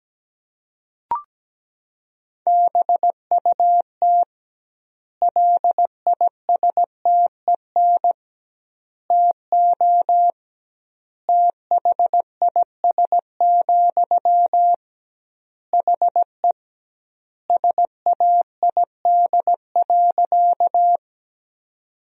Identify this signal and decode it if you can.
Morse code, heard as a keyed tone, about 17 wpm, 700 Hz. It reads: BUT LISTEN TO THIS, HE SAID.
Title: 2x Word-Spacing - Sentences from Top 400 Words - 17wpm